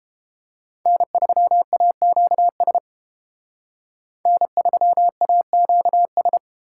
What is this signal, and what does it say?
Morse code, heard as a keyed tone, about 30 wpm, 700 Hz. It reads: N3AQS N3AQS